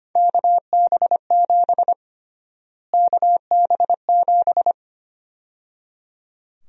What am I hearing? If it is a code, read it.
Morse code, heard as a keyed tone, about 25 wpm, 700 Hz. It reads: KB7 KB7
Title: USA Call Sign Prefixes - Repeated 2x - 25wpm